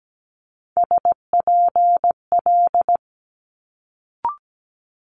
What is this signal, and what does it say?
Morse code, heard as a keyed tone, about 17 wpm, 700 Hz. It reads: SPL
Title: Multi-letter Phonemes Quick Fire 17wpm